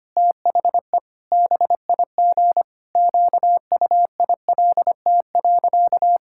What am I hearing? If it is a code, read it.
Morse code, heard as a keyed tone, about 25 wpm, 700 Hz. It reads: THE BIG QUILT.
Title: Pangram Challenge v2 - 25wpm